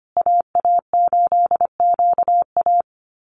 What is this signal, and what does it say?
Morse code, heard as a keyed tone, about 25 wpm, 700 Hz. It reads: AA8QA
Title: USA Call Signs Repeated 3x 25wpm